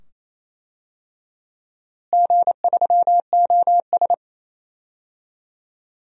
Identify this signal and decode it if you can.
Morse code, heard as a keyed tone, about 28 wpm, 700 Hz. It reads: G3OS
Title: DXCC-100 Call Signs - 4 Characters - Repeated 2x - 28wpm